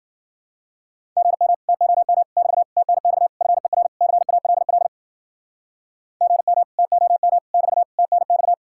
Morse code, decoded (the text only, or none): GM TOM <BT> TNX FER CALL GM TOM <BT> TNX